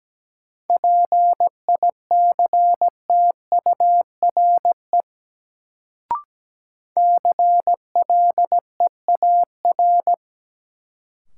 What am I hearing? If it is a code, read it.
Morse code, heard as a keyed tone, about 17 wpm, 700 Hz. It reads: PICTURE CLEAR
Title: Sentences from Top 500 Words - New Word Drill - 17wpm